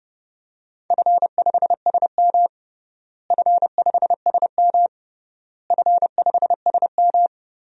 Morse code, heard as a keyed tone, about 32 wpm, 700 Hz. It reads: F5SM F5SM F5SM
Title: DXCC-100 Call Signs - 4 Characters - Repeated 3x - 30wpm